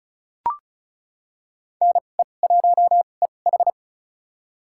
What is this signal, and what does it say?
Morse code, heard as a keyed tone, about 36 wpm, spaced as for 25 wpm, 700 Hz. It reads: NE1EH
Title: USA Call Signs - 2x Character Spacing - 35wpm